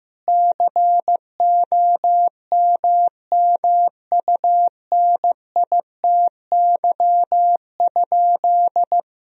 Morse code, heard as a keyed tone, about 15 wpm, 700 Hz. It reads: COMMUNITY?
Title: Sentences from Top 1000 Words - Review Part 4 - 15wpm